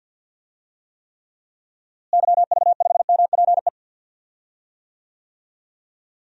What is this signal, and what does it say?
Morse code, heard as a keyed tone, about 50 wpm, 700 Hz. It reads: YV5MPE
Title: DXCC-100 Call Signs - 6 Characters - Repeated 3x - 50wpm